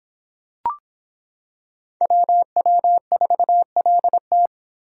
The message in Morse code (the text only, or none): WW4LT